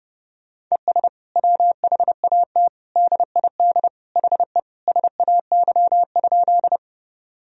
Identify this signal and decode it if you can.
Morse code, heard as a keyed tone, about 32 wpm, 700 Hz. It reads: ES WHAT DID HE SAY?
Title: Sentences from Top 200 Words and Abbreviations 30wpm